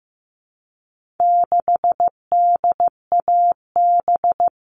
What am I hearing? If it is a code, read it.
Morse code, heard as a keyed tone, about 15 wpm, 700 Hz. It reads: 6DAB
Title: All Cut Numbers - 4-Digit 15wpm